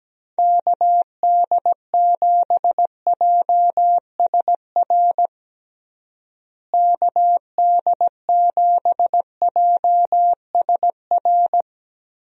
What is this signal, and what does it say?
Morse code, heard as a keyed tone, about 17 wpm, 700 Hz. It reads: KD7JSR KD7JSR
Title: USA Call Signs - 6 Characters - Repeated 3x - 17wpm